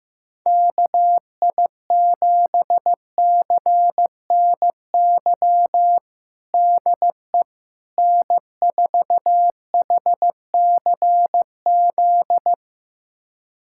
Morse code, heard as a keyed tone, about 15 wpm, 700 Hz. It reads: KI7CNY DE N4HCZ
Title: USA Callsign DE Callsign 15wpm